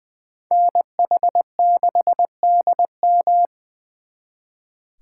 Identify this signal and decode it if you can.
Morse code, heard as a keyed tone, about 25 wpm, 700 Hz. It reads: NH6DM
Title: Speed Racing - USA 2x2 Call Signs 20wpm